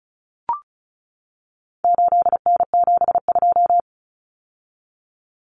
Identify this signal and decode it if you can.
Morse code, heard as a keyed tone, about 34 wpm, 700 Hz. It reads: 8N72
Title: Common Cut Numbers - 4-Digit 35wpm